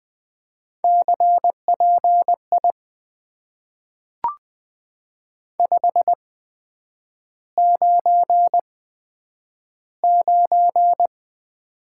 Morse code, heard as a keyed tone, about 20 wpm, spaced as for 9 wpm, 700 Hz. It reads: CPI 599
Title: Top 90 QSO Elements - 8x Character Spacing - 20wpm